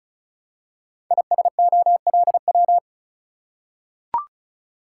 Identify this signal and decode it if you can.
Morse code, heard as a keyed tone, about 36 wpm, 700 Hz. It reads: ISOLW
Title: Code Groups - 5 Letter:Numbers - 35wpm